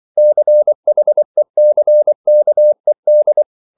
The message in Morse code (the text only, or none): CHECKED